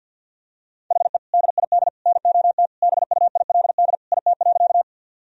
Morse code, heard as a keyed tone, about 50 wpm, 700 Hz. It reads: HE DID NOT BUILD IT.
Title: Sentences from Top 1000 Words - Review Part 1 - 50wpm